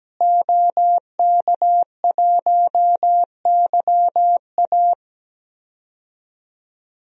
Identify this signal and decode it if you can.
Morse code, heard as a keyed tone, about 17 wpm, 700 Hz. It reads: OK1YA
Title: DXCC-100 Call Signs - 5 Characters - Repeated 2x - 17wpm